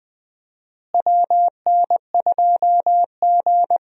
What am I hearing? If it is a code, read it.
Morse code, heard as a keyed tone, about 20 wpm, 700 Hz. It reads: WN2G
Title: USA Call Signs - 4 Characters - Repeated 2x - 20wpm